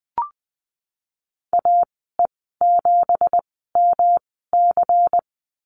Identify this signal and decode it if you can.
Morse code, heard as a keyed tone, about 20 wpm, spaced as for 15 wpm, 700 Hz. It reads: AE7MC
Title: 238 Lesson 38g - Letter q - Call Signs 20wpm